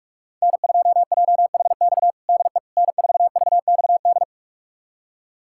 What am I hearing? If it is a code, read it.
Morse code, heard as a keyed tone, about 43 wpm, 700 Hz. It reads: N2JHX DE N4VXD
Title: USA Callsign DE Callsign - 5 Characters - 45wpm